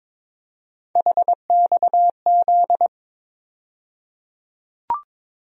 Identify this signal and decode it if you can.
Morse code, heard as a keyed tone, about 22 wpm, 700 Hz. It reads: HXZ
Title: USA Call Sign Postfixes - 3 Letters 22wpm